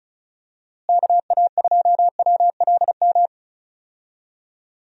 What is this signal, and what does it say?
Morse code, heard as a keyed tone, about 36 wpm, 700 Hz. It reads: KA2WLM
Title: USA Call Signs - 6 Characters - Repeated 3x - 35wpm